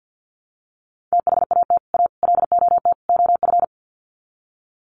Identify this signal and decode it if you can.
Morse code, heard as a keyed tone, about 50 wpm, 700 Hz. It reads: THAT A LOT OF
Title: Sets of 4 Words 50wpm